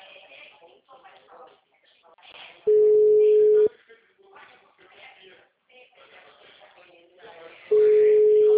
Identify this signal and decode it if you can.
Ringback tone (Europe, 425 Hz), with speech.